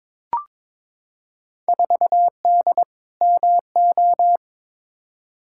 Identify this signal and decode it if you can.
Morse code, heard as a keyed tone, about 22 wpm, 700 Hz. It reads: 4D MO
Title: Field Day Exchanges Encore - 22wpm